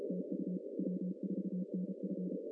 Morse code, heard as a keyed tone, about 33 wpm, 190 Hz. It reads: TU W4NF
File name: CW / cqww-cw-k5tr